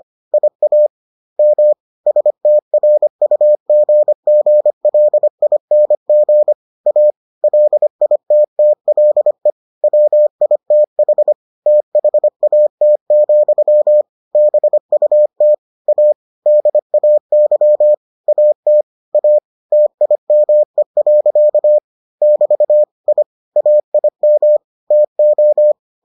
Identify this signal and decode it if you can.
Morse code, heard as a keyed tone, about 25 wpm, 600 Hz. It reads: IAM STRUGGLING A LITTLE WITH THAT, BUT A DAY AT A TIME. <BT> I AIM TO